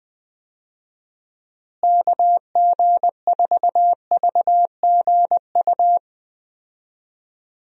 Morse code, heard as a keyed tone, about 20 wpm, 700 Hz. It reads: KG4VGU
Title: USA Call Signs - 6 Characters - Repeated 2x - 20wpm